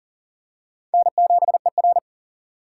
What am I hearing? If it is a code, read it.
Morse code, heard as a keyed tone, about 40 wpm, 700 Hz. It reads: N7ER
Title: USA Call Signs - 4 Characters - 3x Character Spacing - 40wpm